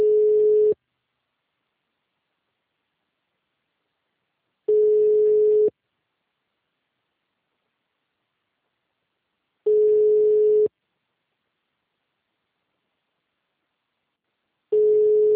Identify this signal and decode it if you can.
Ringback tone (Europe, 425 Hz).
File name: out- -2014- - - .33703